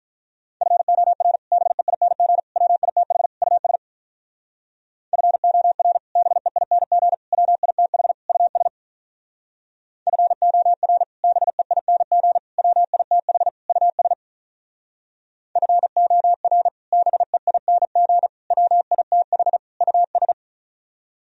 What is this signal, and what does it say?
Morse code, heard as a keyed tone, about 45 wpm, 700 Hz. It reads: FOR BEING WITH US FOR BEING WITH US FOR BEING WITH US FOR BEING WITH US